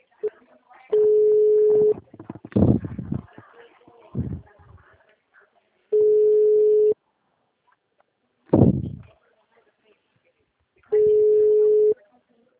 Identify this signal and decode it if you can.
Ringback tone (Europe, 425 Hz), with speech.
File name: out- -2003- - - .0